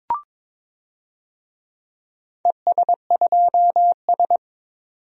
Morse code, heard as a keyed tone, about 22 wpm, 700 Hz. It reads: ES2S
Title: DXCC-100 Call Signs - 4 Characters - 22wpm